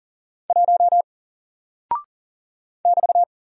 Morse code, heard as a keyed tone, about 39 wpm, 700 Hz. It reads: 1 <BT>
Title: Single Character (Letter:Number:Punctuation:Prosign) 40wpm